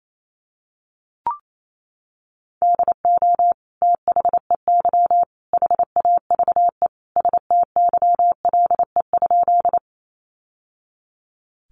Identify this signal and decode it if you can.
Morse code, heard as a keyed tone, about 28 wpm, 700 Hz. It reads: DO THEY HAVE STYLE?
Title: Sentences from Top 800 Words 28wpm